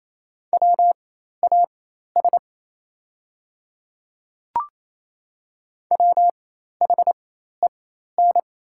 Morse code, heard as a keyed tone, about 28 wpm, spaced as for 13 wpm, 700 Hz. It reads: WAS WHEN
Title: 4x Character Spacing - Top 100 Words - 28wpm